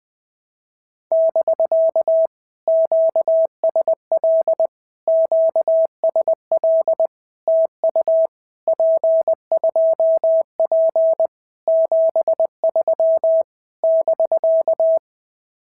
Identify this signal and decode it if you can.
Morse code, heard as a keyed tone, about 21 wpm, 650 Hz. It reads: <BK> QSL QSL TU P2P 73 <BK>
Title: POTA QSO Practice - 20wpm